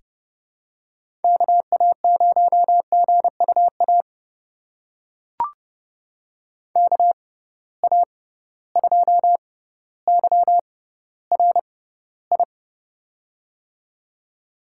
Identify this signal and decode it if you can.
Morse code, heard as a keyed tone, about 32 wpm, spaced as for 16 wpm, 700 Hz. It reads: KA0GUA KA2YRI
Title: USA Call Signs - 6 Characters - 6x Character Spacing - 30wpm